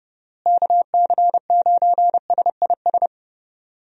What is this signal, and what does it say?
Morse code, heard as a keyed tone, about 30 wpm, 700 Hz. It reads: KC9SIS
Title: USA Call Signs - 6 Characters - Repeated 2x - 30wpm